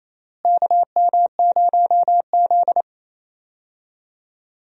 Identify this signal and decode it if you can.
Morse code, heard as a keyed tone, about 29 wpm, 700 Hz. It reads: KM0Z
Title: USA Call Signs - 4 Characters - Repeated 3x - 28wpm